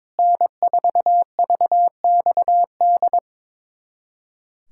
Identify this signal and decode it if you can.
Morse code, heard as a keyed tone, about 21 wpm, 700 Hz. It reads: N4VXD
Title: USA Callsign DE Callsign - 5 Characters - 22wpm